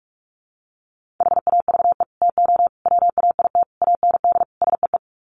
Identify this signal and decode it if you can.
Morse code, heard as a keyed tone, about 44 wpm, 700 Hz. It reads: HAVE TO WAIT AND SEE
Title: Sets of 5 Words Encore2 45wpm